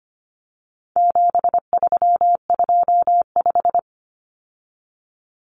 Morse code, heard as a keyed tone, about 25 wpm, 700 Hz. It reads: 7325
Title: Numbers - 4-Digit 25wpm